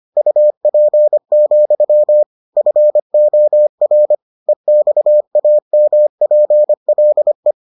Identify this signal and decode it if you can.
Morse code, heard as a keyed tone, about 25 wpm, 600 Hz. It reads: UP, FOR EXAMPLE